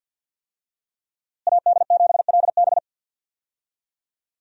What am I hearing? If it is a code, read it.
Morse code, heard as a keyed tone, about 46 wpm, 700 Hz. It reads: AD7LB
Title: USA Call Signs - 5 Characters - Repeated 3x - 50wpm